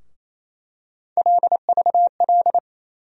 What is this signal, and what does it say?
Morse code, heard as a keyed tone, about 27 wpm, 700 Hz. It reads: LVL
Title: USA Call Sign Postfixes - 3 Letters 28wpm